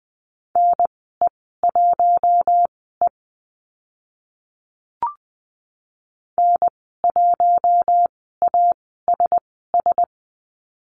Morse code, heard as a keyed tone, about 20 wpm, spaced as for 14 wpm, 700 Hz. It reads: NE1E N1ASS
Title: 042 Lesson 8g - Number 1 - Call Signs 20wpm